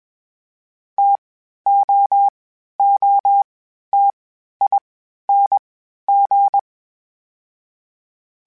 Morse code, heard as a keyed tone, about 21 wpm, 800 Hz. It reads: T O O T I N G
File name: Difficult Combos / Difficult Combos-mp3s / T AOEI @20